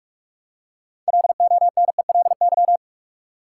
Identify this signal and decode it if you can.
Morse code, heard as a keyed tone, about 45 wpm, 700 Hz. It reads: LONELY
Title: Six Letter Words - 45wpm